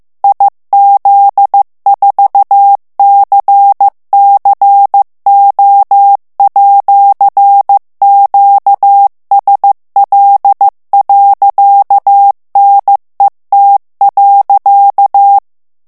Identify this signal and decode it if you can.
Morse code, heard as a keyed tone, about 15 wpm, 800 Hz. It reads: IZ4CCO@QSL.NET.
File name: ITALIEN texte clair 1 vitesse 14 mots minute